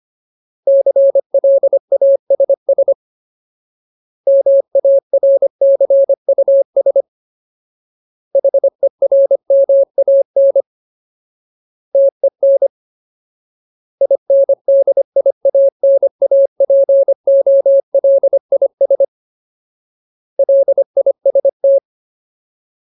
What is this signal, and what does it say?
Morse code, heard as a keyed tone, about 25 wpm, 550 Hz. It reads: CLASS MARCUS HERMAN TEN INDIANAPOLIS LIST
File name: Ultimate Word List / 25WPM Ultimate